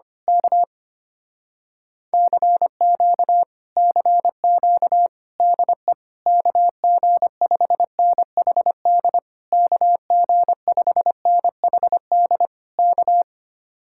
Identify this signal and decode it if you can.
Morse code, heard as a keyed tone, about 24 wpm, 700 Hz. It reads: K CQ CQ DE KG5NHD KG5NHD K
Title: Speed-Racing Common QSO Phrases - 25wpm